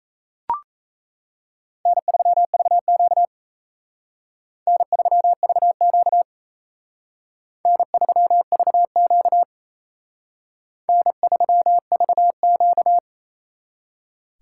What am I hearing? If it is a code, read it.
Morse code, heard as a keyed tone, about 34 wpm, 700 Hz. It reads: N3VQ N3VQ N3VQ N3VQ
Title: Speed Racing - USA 1x2 Call Signs 28wpm